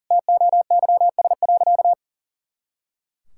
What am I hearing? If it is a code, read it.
Morse code, heard as a keyed tone, about 40 wpm, 700 Hz. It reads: TOYS.
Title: Pangram Challenge v2 - 40wpm